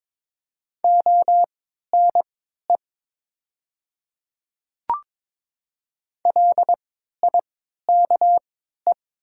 Morse code, heard as a keyed tone, about 22 wpm, spaced as for 13 wpm, 690 Hz. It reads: ONE LIKE
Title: 3x Character Spacing - Top 200 Words - 22wpm